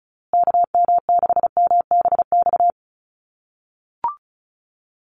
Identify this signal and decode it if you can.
Morse code, heard as a keyed tone, about 34 wpm, 700 Hz. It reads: KM6MBX
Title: USA Call Signs Repeated 3x 35wpm